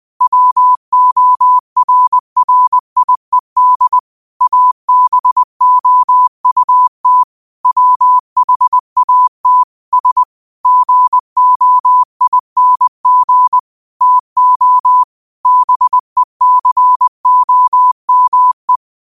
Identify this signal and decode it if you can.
Morse code, heard as a keyed tone, about 20 wpm, 1000 Hz. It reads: WORRIED ABOUT WHAT S GOING TO BECOME
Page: Daily quotes in Morse Code at 20 Words per Minute. Use this to improve your code copying skill.